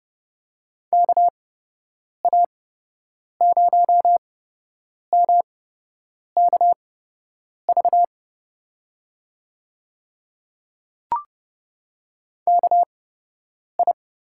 Morse code, heard as a keyed tone, about 30 wpm, spaced as for 8 wpm, 700 Hz. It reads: KA0MKV KI
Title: USA Call Signs - 6 Characters - 8x Character Spacing - 30wpm